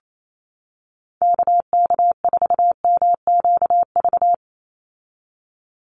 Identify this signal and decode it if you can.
Morse code, heard as a keyed tone, about 28 wpm, 700 Hz. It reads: KK4MQV
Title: USA Call Signs 28wpm